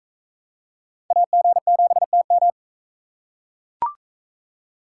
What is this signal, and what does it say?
Morse code, heard as a keyed tone, about 39 wpm, 700 Hz. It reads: AG7TM